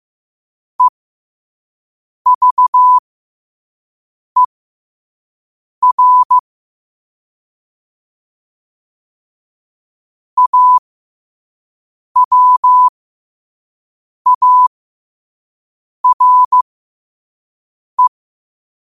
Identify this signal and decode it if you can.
Morse code, heard as a keyed tone, about 15 wpm, spaced as for 5 wpm, 1000 Hz. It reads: EVER AWARE
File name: qotdfiles / 5wpm